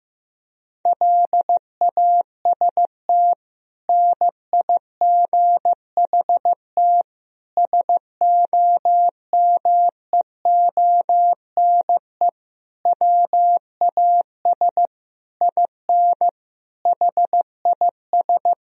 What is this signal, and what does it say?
Morse code, heard as a keyed tone, about 15 wpm, 700 Hz. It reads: LAST NIGHT SOMEONE WAS IN HIS
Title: Sentences from Top 1000 Words - Review Part 4 - 15wpm